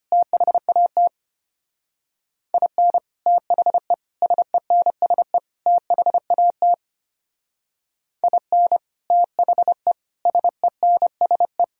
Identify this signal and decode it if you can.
Morse code, heard as a keyed tone, about 30 wpm, 700 Hz. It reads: THAT IN THE SENSE THAT IN THE SENSE
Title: Speed-Racing - Sets of 4 Words 25wpm